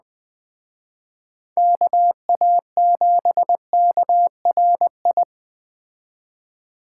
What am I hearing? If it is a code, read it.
Morse code, heard as a keyed tone, about 20 wpm, 700 Hz. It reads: KA7KRI
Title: Speed Racing - USA Call Signs 17wpm